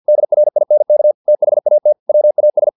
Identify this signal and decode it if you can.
Morse code, heard as a keyed tone, about 50 wpm, 600 Hz. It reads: DRINK THAT WAS
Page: Never 49wpm